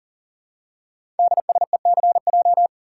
Morse code, heard as a keyed tone, about 39 wpm, 700 Hz. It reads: DSECJ
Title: Code Groups - 5 Letters - 40wpm